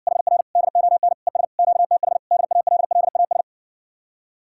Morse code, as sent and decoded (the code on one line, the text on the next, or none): .... .-. / -.. --- -. / . ... / --.- - .... / -.. .- .-.. .-.. .- ...
HR DON ES QTH DALLAS